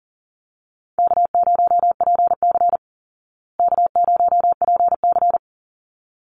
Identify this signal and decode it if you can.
Morse code, heard as a keyed tone, about 40 wpm, 700 Hz. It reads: K0PC K0PC
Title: Easy Call Signs Repeated 2x - 40wpm